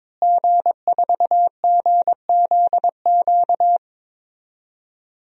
Morse code, heard as a keyed tone, about 22 wpm, 700 Hz. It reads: G4GZQ
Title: DXCC-100 Call Signs - 5 Characters - Repeated 3x - 22wpm